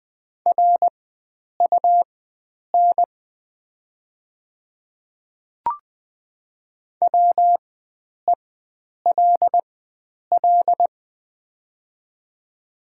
Morse code, heard as a keyed tone, about 20 wpm, spaced as for 9 wpm, 700 Hz. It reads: RUN WELL
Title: 4x Character Spacing - Top 500 Words - 20wpm